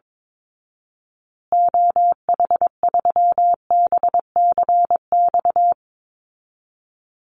Morse code, heard as a keyed tone, about 22 wpm, 700 Hz. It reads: OH3BCX
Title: Call Signs Repeated 2x - 22wpm